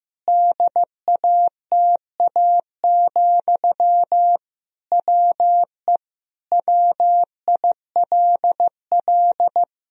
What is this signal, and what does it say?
Morse code, heard as a keyed tone, about 16 wpm, 700 Hz. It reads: DATA, WE WILL
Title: Sentences from Top 1000 Words - Review Part 3 - 15wpm